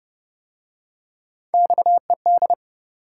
Morse code, heard as a keyed tone, about 33 wpm, 700 Hz. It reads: XED